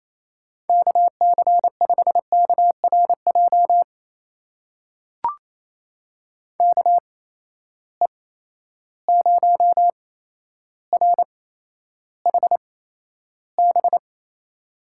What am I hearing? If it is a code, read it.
Morse code, heard as a keyed tone, about 28 wpm, spaced as for 12 wpm, 700 Hz. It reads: KC5KRJ KE0RHB
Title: USA Call Signs - 6 Characters - 8x Character Spacing - 28wpm